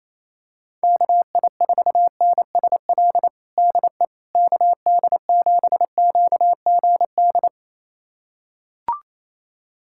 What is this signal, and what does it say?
Morse code, heard as a keyed tone, about 28 wpm, 700 Hz. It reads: KI4NSL DE KD7QGD
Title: USA Callsign DE Callsign - 6 Characters - 28wpm